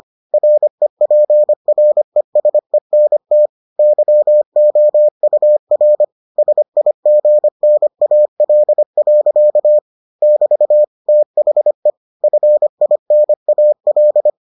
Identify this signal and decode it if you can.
Morse code, heard as a keyed tone, about 25 wpm, 600 Hz. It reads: REPRESENT YOUR SIGNAL. <BT> THE FINAL